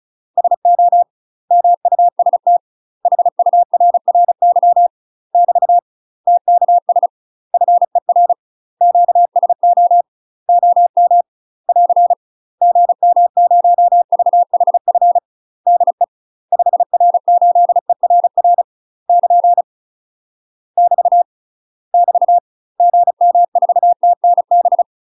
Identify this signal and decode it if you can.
Morse code, heard as a keyed tone, about 35 wpm, 700 Hz. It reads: SO MUST HURRY <BT> TKS FER QSO OM <AR> GM0VHF DE 5R8ERR <KN> <BT> <BT> GM4TNB
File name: UK MASTER QSOs TRAINING / 35WPM / UK MASTER QSOs 0014